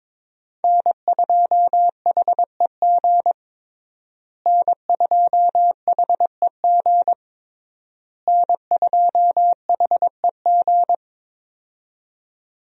Morse code, heard as a keyed tone, about 21 wpm, 700 Hz. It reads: N2HEG N2HEG N2HEG
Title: USA Call Signs - 5 Characters - Repeated 3x - 22wpm